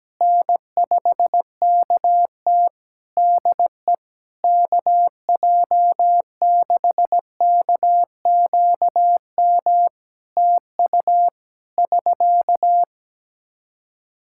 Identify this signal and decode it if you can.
Morse code, heard as a keyed tone, about 17 wpm, 700 Hz. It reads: N5KT DE KJ6KQM TU <SK>
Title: Practice QSO - Exchange 4 - Ending - 17wpm